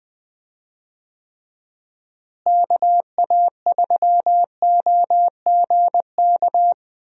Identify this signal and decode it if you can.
Morse code, heard as a keyed tone, about 20 wpm, 700 Hz. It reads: KA3OGK
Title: USA Call Signs - 6 Characters - Repeated 3x - 20wpm